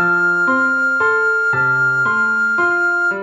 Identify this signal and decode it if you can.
There is no code.